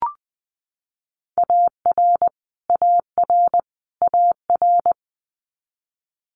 Morse code, heard as a keyed tone, about 20 wpm, 700 Hz. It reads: AR AR AR
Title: Multi-letter Phonemes Repeated 3x 20wpm